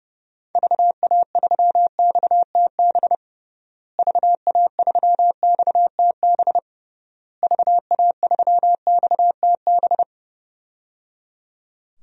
Morse code, heard as a keyed tone, about 30 wpm, 700 Hz. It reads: VA3XTB VA3XTB VA3XTB
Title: DXCC-100 Call Signs - 6 Characters - Repeated 3x - 30wpm